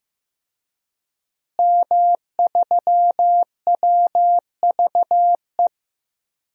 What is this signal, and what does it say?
Morse code, heard as a keyed tone, about 15 wpm, 700 Hz. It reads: M3WVE
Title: DXCC-100 Call Signs - 5 Characters - Repeated 3x - 15wpm